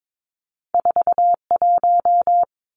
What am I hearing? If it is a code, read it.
Morse code, heard as a keyed tone, about 22 wpm, 700 Hz. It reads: 41
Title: Two Challenging Letter-Number Combinations 22wpm